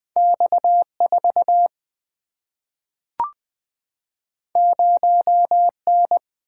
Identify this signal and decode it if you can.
Morse code, heard as a keyed tone, about 20 wpm, 700 Hz. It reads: X4 0N
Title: Single-Letter Single-Number 20wpm